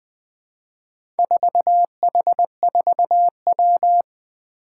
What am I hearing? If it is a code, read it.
Morse code, heard as a keyed tone, about 20 wpm, 700 Hz. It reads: 4H4W
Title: Four Challenging Letter-Number Combinations 20wpm